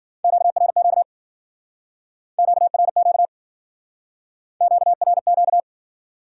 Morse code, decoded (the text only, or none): QRQ QRQ QRQ